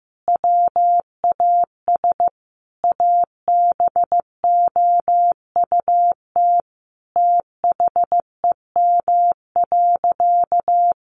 Morse code, heard as a keyed tone, about 15 wpm, 700 Hz. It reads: WAS ABOUT THEM.
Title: Sentences from Top 1500 Words 15wpm